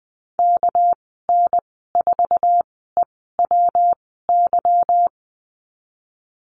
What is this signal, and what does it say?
Morse code, heard as a keyed tone, about 20 wpm, spaced as for 15 wpm, 700 Hz. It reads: KN4EWY
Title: 213 Lesson 34g - Number 8 - Call Signs 20wpm